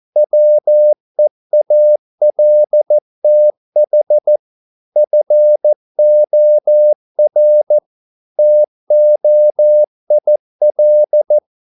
Morse code, heard as a keyed tone, about 14 wpm, 600 Hz. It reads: WEALTH FOR TOIL